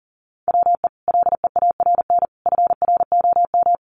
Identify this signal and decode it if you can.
Morse code, heard as a keyed tone, about 40 wpm, 700 Hz. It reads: WE LEARN FROM
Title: Sentences from Top 900 Words 40wpm